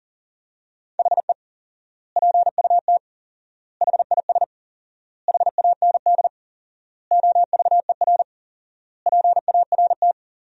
Morse code, heard as a keyed tone, about 40 wpm, 700 Hz. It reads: HE PUT HIS HAND OVER PART
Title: 4x Word-Spacing - Sentences from Top 400 Words - 40wpm